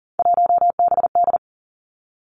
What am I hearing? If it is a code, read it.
Morse code, heard as a keyed tone, about 38 wpm, 700 Hz. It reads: 1BD
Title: Three Challenging Letter-Number Combinations 40wpm